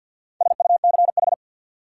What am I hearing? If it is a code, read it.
Morse code, heard as a keyed tone, about 52 wpm, 700 Hz. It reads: SUCH
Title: Sets of 2 Words - New Word Drill - 50wpm